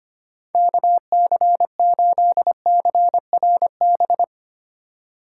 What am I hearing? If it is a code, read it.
Morse code, heard as a keyed tone, about 25 wpm, 700 Hz. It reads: KC8CRB
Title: USA Call Signs - 6 Characters - 3x Character Spacing - 25wpm